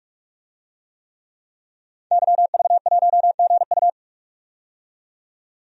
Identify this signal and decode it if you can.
Morse code, heard as a keyed tone, about 48 wpm, 700 Hz. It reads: YV1GU
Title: DXCC-100 Call Signs - 5 Characters - 45wpm